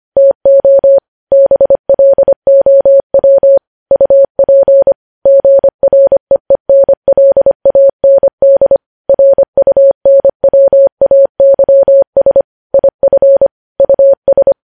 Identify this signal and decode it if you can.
Morse code, heard as a keyed tone, about 25 wpm, 550 Hz. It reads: TO BLOW UP GREENLAND RUNWAYS IF US
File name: podcast / News headlines-1x-25x25